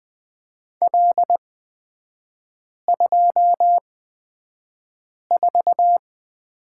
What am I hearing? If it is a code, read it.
Morse code, heard as a keyed tone, about 20 wpm, spaced as for 5 wpm, 700 Hz. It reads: L24
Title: 236 Lesson 38e - Letter q - Three Characters 20wpm